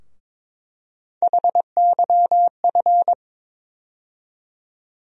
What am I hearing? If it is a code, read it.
Morse code, heard as a keyed tone, about 21 wpm, 700 Hz. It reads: HYF
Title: USA Call Sign Postfixes - 3 Letters 22wpm